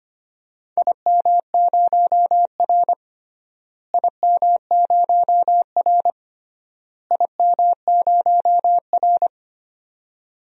Morse code, heard as a keyed tone, about 25 wpm, 700 Hz. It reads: IM0R IM0R IM0R
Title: DXCC-100 Call Signs - 4 Characters - Repeated 3x - 25wpm